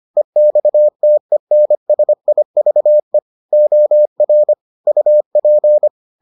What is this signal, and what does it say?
Morse code, heard as a keyed tone, about 25 wpm, 600 Hz. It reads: EXTENSIVE OR UP